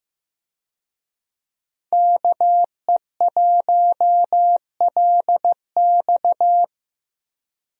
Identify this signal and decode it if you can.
Morse code, heard as a keyed tone, about 15 wpm, 700 Hz. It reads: KE1LX
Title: USA Call Signs - 5 Characters - 15wpm